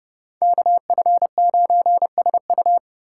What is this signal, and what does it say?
Morse code, heard as a keyed tone, about 33 wpm, 700 Hz. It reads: KF9SU